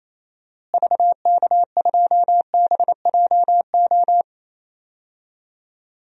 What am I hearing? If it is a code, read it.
Morse code, heard as a keyed tone, about 28 wpm, 700 Hz. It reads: VK2BJO
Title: DXCC-100 Call Signs - 6 Characters - Repeated 2x - 28wpm